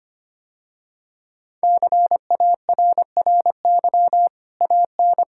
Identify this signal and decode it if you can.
Morse code, heard as a keyed tone, about 25 wpm, 700 Hz. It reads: CARRY AN
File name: Easy Sentences with Hints - 25wpm